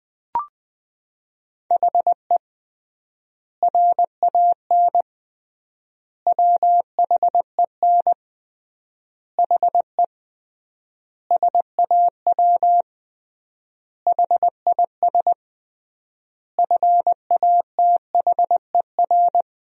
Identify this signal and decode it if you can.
Morse code, heard as a keyed tone, about 20 wpm, 700 Hz. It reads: HE RAN WHEN HE SAW HIS FATHER
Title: 3x Word-Spacing - Sentences from Top 500 Words - 20wpm